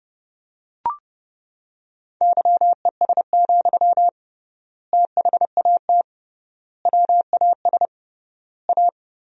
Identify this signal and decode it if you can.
Morse code, heard as a keyed tone, about 30 wpm, 700 Hz. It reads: YES, THAT WAS A
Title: 3x Word-Spacing - Sentences from Top 500 Words - 30wpm